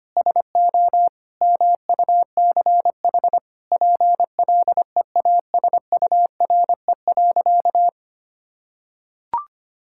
Morse code, heard as a keyed tone, about 25 wpm, 700 Hz. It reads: SO MUCH PLEASURE.
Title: Sentences from Top 1600 Words 25wpm